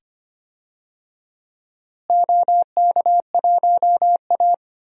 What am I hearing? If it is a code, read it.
Morse code, heard as a keyed tone, about 25 wpm, 700 Hz. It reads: OK1A
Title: DXCC-100 Call Signs - 4 Characters - Repeated 2x - 25wpm